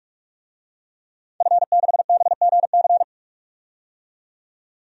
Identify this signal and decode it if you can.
Morse code, heard as a keyed tone, about 46 wpm, 700 Hz. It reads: F6BGC
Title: DXCC-100 Call Signs - 5 Characters - 45wpm